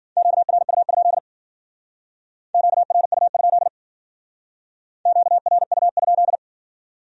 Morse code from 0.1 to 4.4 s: QRU? QRU?

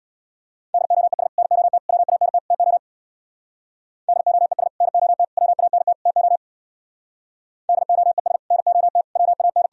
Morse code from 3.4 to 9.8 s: DOES NOT WANT TO DOES NOT WAN

